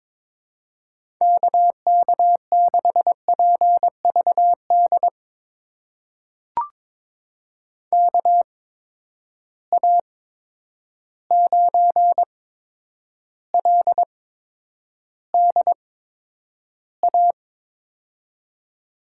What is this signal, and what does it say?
Morse code, heard as a keyed tone, about 22 wpm, spaced as for 9 wpm, 700 Hz. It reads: KK6PVD KA9LDA
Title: USA Call Signs - 6 Characters - 8x Character Spacing - 22wpm